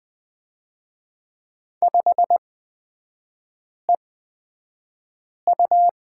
Morse code, heard as a keyed tone, about 20 wpm, spaced as for 5 wpm, 700 Hz. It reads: 5EU
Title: 243 Lesson 39e - Letter z - Three Characters 20wpm